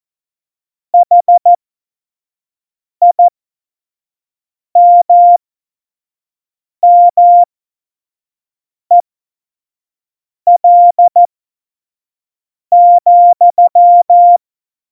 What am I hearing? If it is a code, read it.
Morse code, heard as a keyed tone, about 14 wpm, spaced as for 5 wpm, 700 Hz. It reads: HIMMEL,
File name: Morsekurs / Witze / Witz 17.Juli 2018, 25 F70